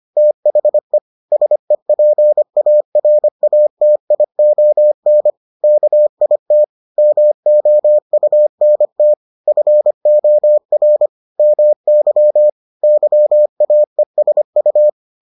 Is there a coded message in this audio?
Yes: Morse code from 0.2 to 14.9 s, heard as a keyed tone, about 25 wpm, 600 Hz. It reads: THE SEPARATION KIT MOUNT FOR MY YAESU